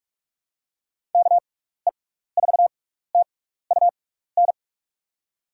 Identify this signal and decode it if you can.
Morse code, heard as a keyed tone, about 43 wpm, spaced as for 15 wpm, 700 Hz. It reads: KE4TUN
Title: USA Call Signs - 6 Characters - 6x Character Spacing - 45wpm